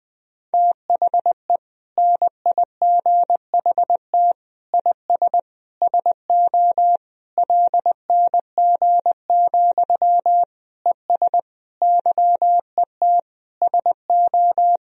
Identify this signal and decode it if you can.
Morse code, heard as a keyed tone, about 20 wpm, 700 Hz. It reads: THE NIGHT IS SO LNG, ES YET SO
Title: Sentences from Top 500 Words and Abbreviations 20wpm